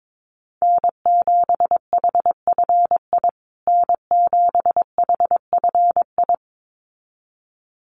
Morse code, heard as a keyed tone, about 22 wpm, 700 Hz. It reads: N7HFI N7HFI
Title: USA Call Signs Repeated 2x 22wpm